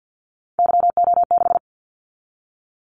Morse code, heard as a keyed tone, about 50 wpm, 700 Hz. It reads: YO6
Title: Callsign Prefixes for Top 25 DXCC - 50wpm